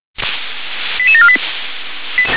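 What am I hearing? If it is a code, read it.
Five-tone selective call (ZVEI), heard as quick runs of five tones, tones of about 80 ms. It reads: 88748
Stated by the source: ZVEI five-tone selective call 88856, received by SDR.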